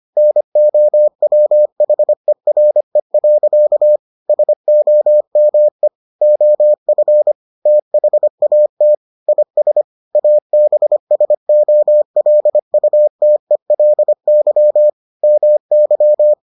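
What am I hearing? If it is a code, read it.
Morse code, heard as a keyed tone, about 25 wpm, 600 Hz. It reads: NOWHERE. SOME OF THAT IS ABSOLUTELY MY
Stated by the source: Morse edition of the Foundations of Amateur Radio episodes.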